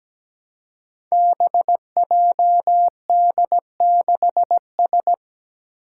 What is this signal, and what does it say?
Morse code, heard as a keyed tone, about 17 wpm, 700 Hz. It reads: BJD6S
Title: Five Challenging Letter-Number Combinations 17wpm